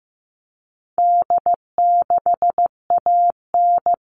Morse code, heard as a keyed tone, about 15 wpm, 700 Hz. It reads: D6AN
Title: All Cut Numbers - 4-Digit 15wpm